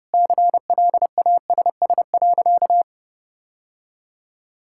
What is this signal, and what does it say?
Morse code, heard as a keyed tone, about 30 wpm, 700 Hz. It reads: CLASS.
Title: Sentences from Top 1000 Words - Review Part 6 - 30wpm